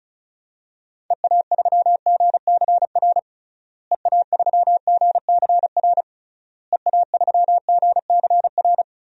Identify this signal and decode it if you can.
Morse code, heard as a keyed tone, about 36 wpm, 700 Hz. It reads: EA3GCR EA3GCR EA3GCR
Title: DXCC-100 Call Signs - 6 Characters - Repeated 3x - 35wpm